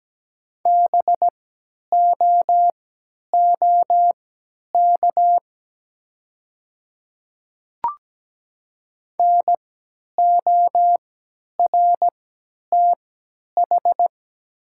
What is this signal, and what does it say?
Morse code, heard as a keyed tone, about 17 wpm, spaced as for 10 wpm, 700 Hz. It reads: BOOK NORTH
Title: 3x Character Spacing - Top 500 Words - 17wpm